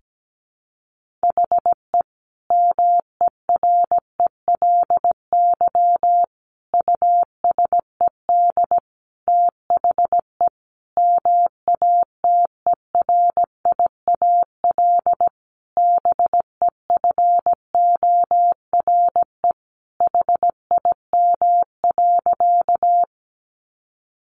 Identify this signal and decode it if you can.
Morse code, heard as a keyed tone, about 17 wpm, 700 Hz. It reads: HE MERELY USED THE MATERIAL BEFORE HIM.
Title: Sentences from Top 1400 Words 17wpm